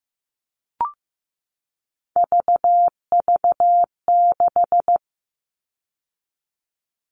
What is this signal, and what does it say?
Morse code, heard as a keyed tone, about 15 wpm, 700 Hz. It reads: VV6
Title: All Cut Numbers - 3-Digit 15wpm